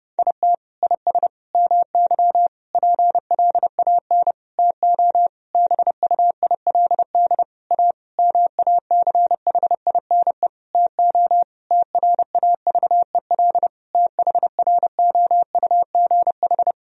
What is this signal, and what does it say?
Morse code, heard as a keyed tone, about 30 wpm, 700 Hz. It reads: IT IS MY PLAN TO BUILD A MACHINE TO TRAVEL THROUGH
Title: Sentences from Top 1000 Words - Review Part 2 - 30wpm